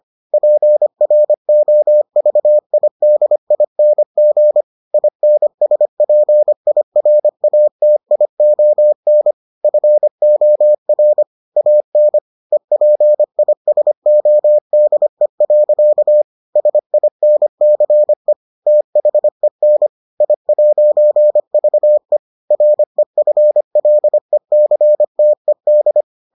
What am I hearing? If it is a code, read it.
Morse code, heard as a keyed tone, about 25 wpm, 600 Hz. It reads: PROVIDING INSPIRATION FOR AN EPISODE. SINCE THEN I'VE REFLECTED